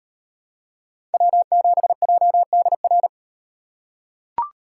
Morse code, heard as a keyed tone, about 37 wpm, 700 Hz. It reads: W7JDR